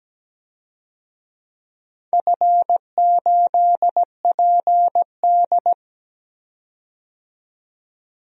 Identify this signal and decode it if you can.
Morse code, heard as a keyed tone, about 17 wpm, 700 Hz. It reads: F8PD
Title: DXCC-100 Call Signs - 4 Characters - 17wpm